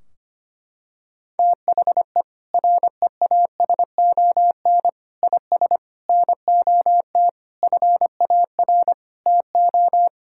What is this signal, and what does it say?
Morse code, heard as a keyed tone, about 25 wpm, 700 Hz. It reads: THE REASON IS NOT FAR TO
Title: Sentences from Top 1600 Words 25wpm